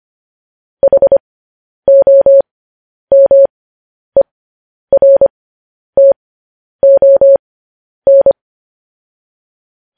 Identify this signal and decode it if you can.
Morse code, heard as a keyed tone, about 25 wpm, spaced as for 10 wpm, 550 Hz. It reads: HOMERTON
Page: news-headlines-in-morse-code-at-10-wpm